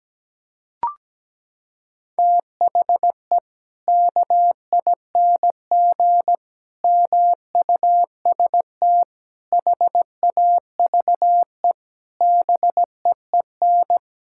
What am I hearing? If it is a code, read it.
Morse code, heard as a keyed tone, about 17 wpm, 700 Hz. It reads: THE KING MUST HAVE BEEN
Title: Sentences from Top 1000 Words - Review Part 5 - 17wpm